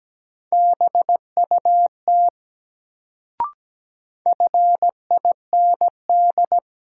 Morse code, heard as a keyed tone, about 17 wpm, 700 Hz. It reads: BUT FIND
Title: Sets of 2 Words - New Word Drill - 17wpm